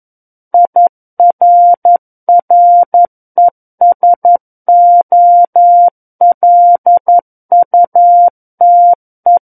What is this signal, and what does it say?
Morse code, heard as a keyed tone, about 11 wpm, 700 Hz. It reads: IRRESOLUTE
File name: Book One / 02-Chapter 2 - 11 WPM